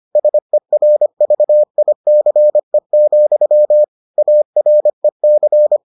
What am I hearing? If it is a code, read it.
Morse code, heard as a keyed tone, about 25 wpm, 600 Hz. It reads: SERVICE, AREC